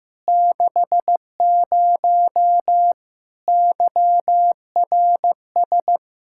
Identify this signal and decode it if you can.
Morse code, heard as a keyed tone, about 15 wpm, 700 Hz. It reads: 60 YRS
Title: Practice QSO - Exchange 3 - More Info - 15wpm